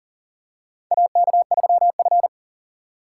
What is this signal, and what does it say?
Morse code, heard as a keyed tone, about 38 wpm, 700 Hz. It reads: AK3F
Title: USA Call Signs - 4 Characters - 3x Character Spacing - 40wpm